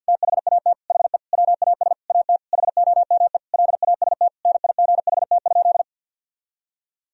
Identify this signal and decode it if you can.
Morse code, heard as a keyed tone, about 52 wpm, 700 Hz. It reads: THAT HE WAS AT HOME LAST NIGHT?
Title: Sentences from Top 1000 Words - Review Part 2 - 50wpm